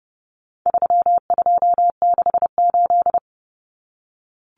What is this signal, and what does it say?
Morse code, heard as a keyed tone, about 30 wpm, 700 Hz. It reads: 3268
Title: Numbers - 4-Digit 30wpm